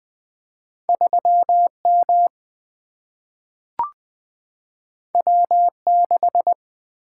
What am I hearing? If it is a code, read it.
Morse code, heard as a keyed tone, about 20 wpm, 700 Hz. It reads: 3M W6
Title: Single-Letter Single-Number 20wpm